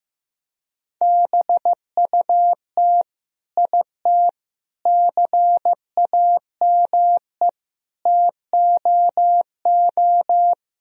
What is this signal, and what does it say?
Morse code, heard as a keyed tone, about 15 wpm, 700 Hz. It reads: BUT IT CAME TOO
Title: Sentences from Top 1000 Words - Review Part 2 - 15wpm